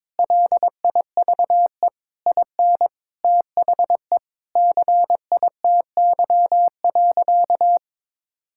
Morse code, heard as a keyed tone, about 23 wpm, 700 Hz. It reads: LIVE IN THE CITY.
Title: Sentences from Top 1000 Words - Review Part 6 - 22wpm